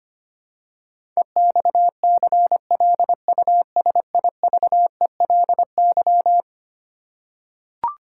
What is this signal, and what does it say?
Morse code, heard as a keyed tone, about 25 wpm, 700 Hz. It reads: EXCLUSIVELY
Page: morse-code-ninja-practice-25wpm